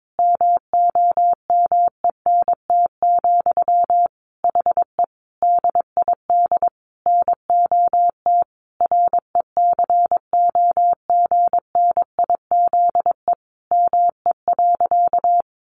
Morse code, heard as a keyed tone, about 22 wpm, 700 Hz. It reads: MOMENT, HE DID NOT RECOGNIZE ME.
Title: Sentences from Top 1400 Words 22wpm